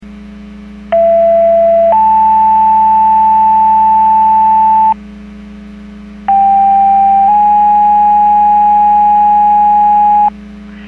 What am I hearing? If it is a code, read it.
Two-tone page, heard as a one-second tone, then a three-second tone, about 670 then 880 Hz, about 790 then 830 Hz. It is heard from 0.9 to 10.3 s.
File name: Portage 154.385 Paging